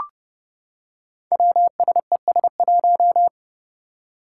Morse code, heard as a keyed tone, about 30 wpm, 700 Hz. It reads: WSES1
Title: Five Challenging Letter-Number Combinations 30wpm